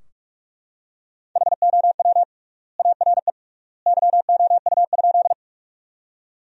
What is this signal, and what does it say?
Morse code, heard as a keyed tone, about 41 wpm, 700 Hz. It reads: HOW ARE YOU?
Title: 3x Word-Spacing - Sentences from Top 100 Words - 45wpm